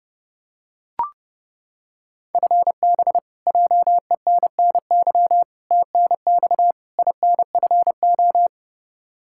Morse code, heard as a keyed tone, about 30 wpm, 700 Hz. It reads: FB JENNY TNX INFO
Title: K1USN SST QSO Practice 30wpm